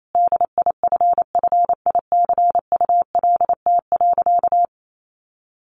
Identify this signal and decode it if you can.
Morse code, heard as a keyed tone, about 28 wpm, 700 Hz. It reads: DIFFICULT.
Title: Sentences from Top 1500 Words 28wpm